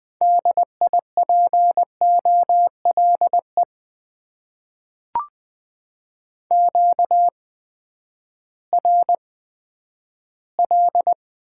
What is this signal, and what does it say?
Morse code, heard as a keyed tone, about 20 wpm, spaced as for 11 wpm, 700 Hz. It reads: DIPOLE QRL
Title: Top 90 QSO Elements - 8x Character Spacing - 20wpm